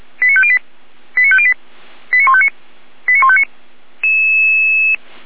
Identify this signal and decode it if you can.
Five-tone selective call (ZVEI), heard as quick runs of five tones, tones of about 70 ms. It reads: 89508 89508 89159 89159